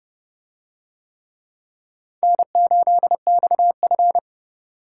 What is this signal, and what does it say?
Morse code, heard as a keyed tone, about 28 wpm, 700 Hz. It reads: N8XF